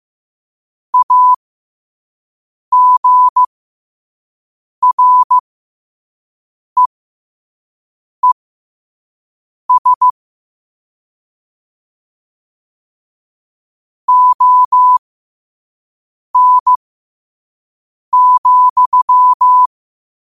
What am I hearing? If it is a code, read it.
Morse code, heard as a keyed tone, about 15 wpm, spaced as for 5 wpm, 1000 Hz. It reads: AGREES ON,
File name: qotdfiles / 5wpm